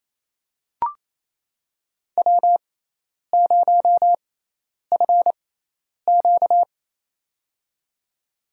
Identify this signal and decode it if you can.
Morse code, heard as a keyed tone, about 28 wpm, spaced as for 10 wpm, 700 Hz. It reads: W0FQ
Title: USA Call Signs - 4 Characters - 6x Character Spacing - 28wpm